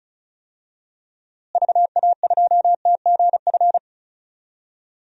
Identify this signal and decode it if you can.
Morse code, heard as a keyed tone, about 34 wpm, 700 Hz. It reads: VA2TGF
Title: DXCC-100 Call Signs - 6 Characters - Repeated 3x - 35wpm